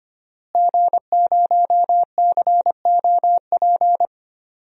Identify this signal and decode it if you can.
Morse code, heard as a keyed tone, about 25 wpm, 700 Hz. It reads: G0COP
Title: DXCC-100 Call Signs - 5 Characters - Repeated 3x - 25wpm